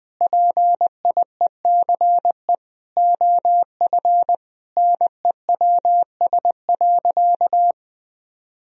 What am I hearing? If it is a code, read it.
Morse code, heard as a keyed tone, about 19 wpm, 700 Hz. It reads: PIECE OF NEWS.
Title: Sentences from Top 1000 Words - Review Part 2 - 20wpm